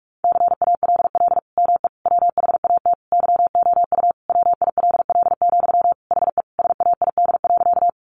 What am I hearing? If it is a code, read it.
Morse code, heard as a keyed tone, about 44 wpm, 700 Hz. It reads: CALL ME WHAT YOU WILL, HE SAID.
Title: Sentences from Top 200 Words 45wpm